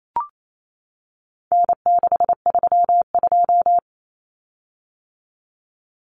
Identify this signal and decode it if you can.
Morse code, heard as a keyed tone, about 28 wpm, 700 Hz. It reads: N632
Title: Common Cut Numbers - 4-Digit 28wpm